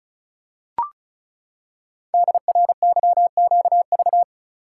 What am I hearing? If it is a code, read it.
Morse code, heard as a keyed tone, about 36 wpm, 700 Hz. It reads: DRYQV
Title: Code Groups - 5 Letters - 35wpm